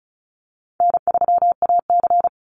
Morse code, heard as a keyed tone, about 34 wpm, 700 Hz. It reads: N3AC
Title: Easy Call Signs Repeated 2x - 35wpm